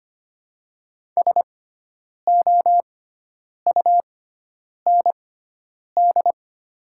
Morse code, heard as a keyed tone, about 25 wpm, spaced as for 9 wpm, 700 Hz. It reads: SOUND
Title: 6x Character Spacing - Top 200 Words - 25wpm